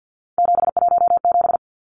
Morse code, heard as a keyed tone, about 50 wpm, 700 Hz. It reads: 717
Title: Numbers - 3-Digit 50wpm